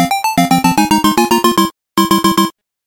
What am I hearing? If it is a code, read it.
There is no code.